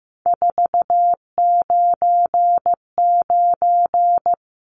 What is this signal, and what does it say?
Morse code, heard as a keyed tone, about 15 wpm, 700 Hz. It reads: 499
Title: Numbers - 3-Digit 15wpm